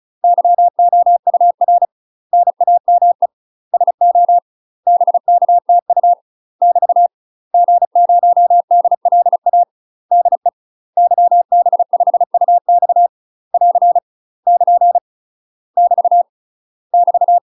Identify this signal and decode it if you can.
Morse code, heard as a keyed tone, about 36 wpm, 700 Hz. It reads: YOUR NAME SO BKTU <BT> G0DLA DE YB5UX <AR> <KN> <BT> <BT>
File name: UK MASTER QSOs TRAINING / 35WPM / UK MASTER QSOs 0020